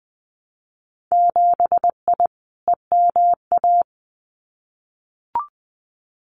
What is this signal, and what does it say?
Morse code, heard as a keyed tone, about 19 wpm, 700 Hz. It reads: 7I EMA
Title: Winter Field Day Exchanges Encore 20wpm